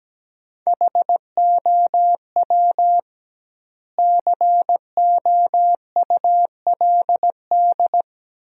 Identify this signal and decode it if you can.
Morse code, heard as a keyed tone, about 17 wpm, 700 Hz. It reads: HOW COULD
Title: 2x Word-Spacing - Sentences from Top 400 Words - 17wpm